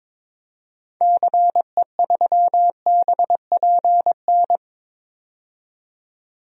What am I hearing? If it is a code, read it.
Morse code, heard as a keyed tone, about 22 wpm, 700 Hz. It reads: CE3BPN
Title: DXCC-100 Call Signs - 6 Characters - 22wpm